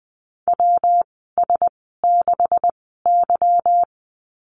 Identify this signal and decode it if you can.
Morse code, heard as a keyed tone, about 20 wpm, spaced as for 15 wpm, 700 Hz. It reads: WS6Y
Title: 147 Lesson 24g - Letter y - Call Signs 20wpm